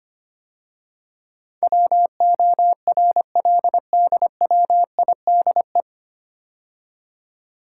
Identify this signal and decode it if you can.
Morse code, heard as a keyed tone, about 25 wpm, 700 Hz. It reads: WORLDWIDE
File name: practice / 25wpm / 9 Letter Words - First 3 Letters Given - 25wpm